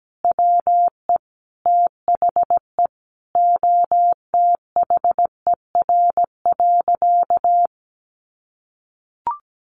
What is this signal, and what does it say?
Morse code, heard as a keyed tone, about 17 wpm, 700 Hz. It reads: WE THE OTHER.
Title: Sentences from Top 1300 Words 17wpm